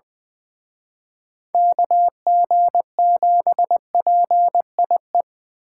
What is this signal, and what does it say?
Morse code, heard as a keyed tone, about 24 wpm, 700 Hz. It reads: KG7PIE
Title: Speed Racing - USA 2x3 Call Signs 20wpm